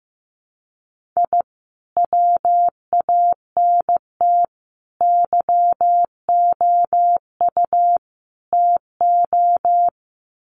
Morse code, heard as a keyed tone, about 14 wpm, 700 Hz. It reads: I WANT YOU TO
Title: Sets of 4 Words 15wpm